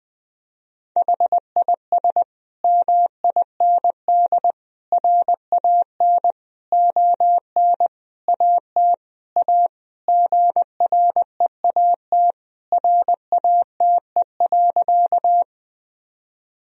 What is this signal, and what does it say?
Morse code, heard as a keyed tone, about 20 wpm, 700 Hz. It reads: HIS MIND RAN ON AT A GREAT RATE.
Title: Sentences from Top 1000 Words - Review Part 4 - 20wpm